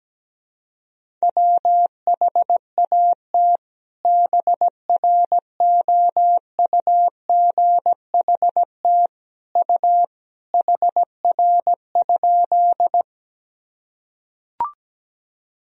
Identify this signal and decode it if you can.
Morse code, heard as a keyed tone, about 17 wpm, 700 Hz. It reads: WHAT BROUGHT U HR?
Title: Sentences from Top 500 Words and Abbreviations 17wpm